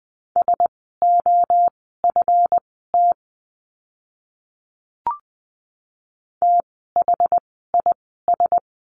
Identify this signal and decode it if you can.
Morse code, heard as a keyed tone, about 20 wpm, spaced as for 14 wpm, 700 Hz. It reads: SOFT THIS
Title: 139 Lesson 23f - Letter f - Words 20wpm